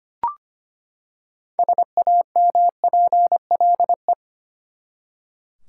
Morse code, heard as a keyed tone, about 25 wpm, 700 Hz. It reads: SAMPLE
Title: Six Letter Words - 25wpm